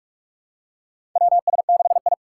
Morse code, heard as a keyed tone, about 45 wpm, 700 Hz. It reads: WS6I
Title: USA Call Signs - 4 Characters - 8x Character Spacing - 45wpm